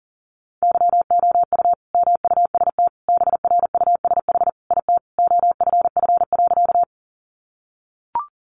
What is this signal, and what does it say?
Morse code, heard as a keyed tone, about 40 wpm, 700 Hz. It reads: YOU MUST BRUSH IT OFF.
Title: Sentences from Top 1100 Words 40wpm